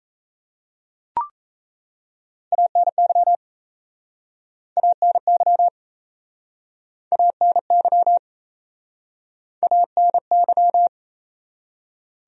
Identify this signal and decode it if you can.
Morse code, heard as a keyed tone, about 39 wpm, 700 Hz. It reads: ANY ANY ANY ANY